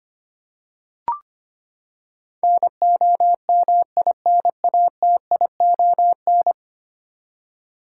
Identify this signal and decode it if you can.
Morse code, heard as a keyed tone, about 25 wpm, 700 Hz. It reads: NOMINATION
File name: practice / 25wpm / Ten Letter Words - 25wpm